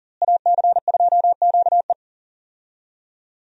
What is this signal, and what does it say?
Morse code, heard as a keyed tone, about 40 wpm, 700 Hz. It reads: AC2QE
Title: USA Call Signs - 5 Characters - Repeated 2x - 40wpm